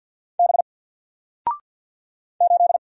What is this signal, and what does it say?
Morse code, heard as a keyed tone, about 50 wpm, 700 Hz. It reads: B 8
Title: Single Letter-Number 50wpm